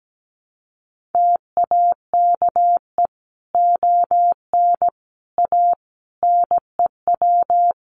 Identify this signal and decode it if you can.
Morse code, heard as a keyed tone, about 17 wpm, 700 Hz. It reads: TAKE ON A NEW
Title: Sentences from Top 1400 Words 17wpm